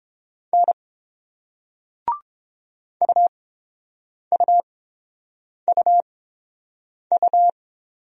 Morse code, heard as a keyed tone, about 30 wpm, spaced as for 7 wpm, 690 Hz. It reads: N UUUU